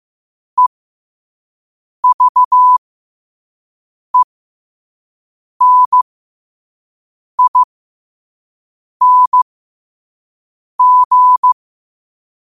Morse code, heard as a keyed tone, about 15 wpm, spaced as for 5 wpm, 1000 Hz. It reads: EVENING
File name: qotdfiles / 5wpm